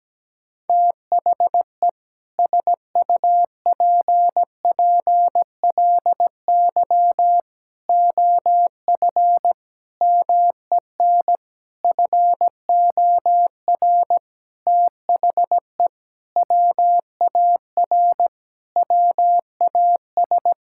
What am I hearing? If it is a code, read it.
Morse code, heard as a keyed tone, about 17 wpm, 700 Hz. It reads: THE SUPPLY OF MEN FOR THE WAR WAS
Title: Sentences from Top 1000 Words - Review Part 6 - 17wpm